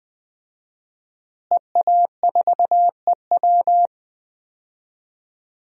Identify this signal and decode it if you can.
Morse code, heard as a keyed tone, about 20 wpm, 700 Hz. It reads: EA4EW
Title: DXCC-100 Call Signs - 5 Characters - Repeated 2x - 20wpm